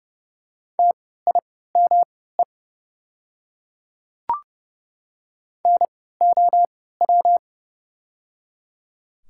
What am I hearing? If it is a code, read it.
Morse code, heard as a keyed tone, about 30 wpm, spaced as for 17 wpm, 690 Hz. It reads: TIME NOW
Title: 3x Character Spacing - Top 200 Words - 30wpm